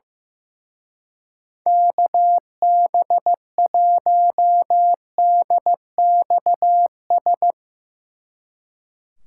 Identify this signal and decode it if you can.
Morse code, heard as a keyed tone, about 15 wpm, 700 Hz. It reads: KB1DXS
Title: Speed Racing - USA 2x3 Call Signs 15wpm